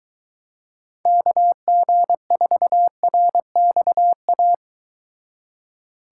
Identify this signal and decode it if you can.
Morse code, heard as a keyed tone, about 23 wpm, 700 Hz. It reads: KG4RXA